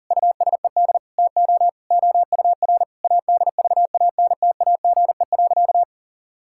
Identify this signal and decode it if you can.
Morse code, heard as a keyed tone, about 40 wpm, 700 Hz. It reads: USED TO OUR ADVANTAGE.
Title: Sentences from Top 1000 Words - Review Part 6 - 40wpm